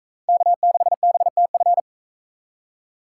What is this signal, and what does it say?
Morse code, heard as a keyed tone, about 40 wpm, 700 Hz. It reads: K6BTF